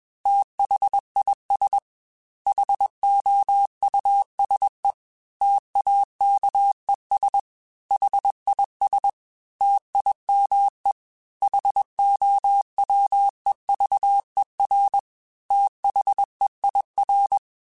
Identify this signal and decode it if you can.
Morse code, heard as a keyed tone, about 21 wpm, 800 Hz. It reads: THIS HOUSE TAKES HIS TIME HOWEVER THEIR
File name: T then H Runs / T then H Runs.mp3s / T then H Runs @20